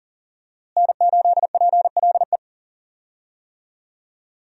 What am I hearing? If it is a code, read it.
Morse code, heard as a keyed tone, about 40 wpm, 690 Hz. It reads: N8PLE
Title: USA Call Signs - 5 Characters - Repeated 3x - 40wpm